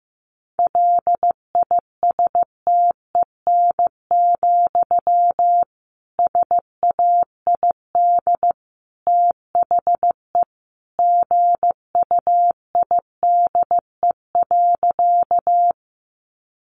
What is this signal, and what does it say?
Morse code, heard as a keyed tone, about 15 wpm, 700 Hz. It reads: LISTEN, SAID THE GUIDE.
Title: Sentences from Top 600 Words 15wpm